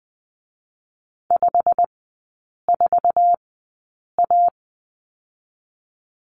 Morse code, heard as a keyed tone, about 20 wpm, 700 Hz. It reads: 5 4 A
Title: 114 Lesson 19e - Letter w - Three Characters 20wpm